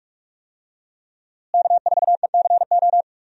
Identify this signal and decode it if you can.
Morse code, heard as a keyed tone, about 46 wpm, 700 Hz. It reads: K4ECO
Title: USA Call Signs - 5 Characters - Repeated 3x - 45wpm